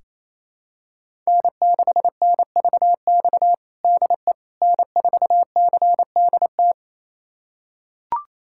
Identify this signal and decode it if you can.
Morse code, heard as a keyed tone, about 28 wpm, 700 Hz. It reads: N6NVX DE N4CDT
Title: USA Callsign DE Callsign - 5 Characters - 28wpm